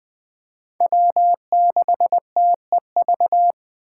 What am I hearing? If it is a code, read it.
Morse code, heard as a keyed tone, about 23 wpm, 700 Hz. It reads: W6TEV
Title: Speed Racing - USA 1x3 Call Signs 20wpm